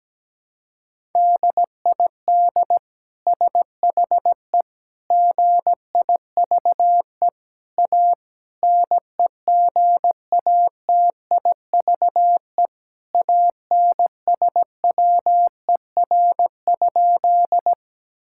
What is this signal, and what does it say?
Morse code, heard as a keyed tone, about 17 wpm, 700 Hz. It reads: DID SHE GIVE A NEGATIVE ANSWER?
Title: Sentences from Top 1000 Words - Review Part 6 - 17wpm